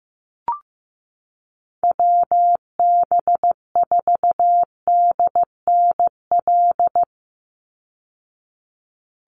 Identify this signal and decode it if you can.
Morse code, heard as a keyed tone, about 15 wpm, 700 Hz. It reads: WB4DNL
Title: Call Signs Repeated 2x - 15wpm